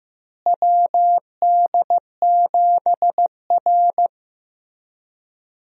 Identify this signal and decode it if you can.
Morse code, heard as a keyed tone, about 15 wpm, 700 Hz. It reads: WD7R
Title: USA Call Signs - 4 Characters - Repeated 3x - 15wpm